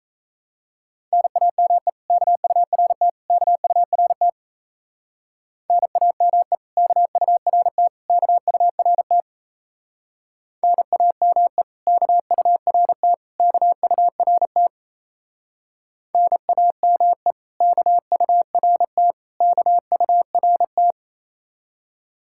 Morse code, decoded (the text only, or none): NAME KURT KURT NAME KURT KURT NAME KURT KURT NAME KURT KURT